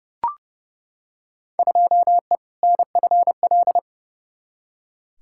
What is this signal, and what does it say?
Morse code, heard as a keyed tone, about 31 wpm, 700 Hz. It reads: 2E NFL
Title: Field Day Exchanges Encore - 30wpm